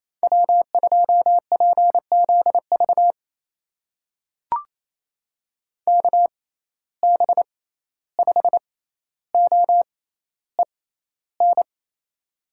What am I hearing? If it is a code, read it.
Morse code, heard as a keyed tone, about 28 wpm, spaced as for 14 wpm, 700 Hz. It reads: W2PZV KB5OEN
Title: USA Call Signs - 6x Character Spacing - 28wpm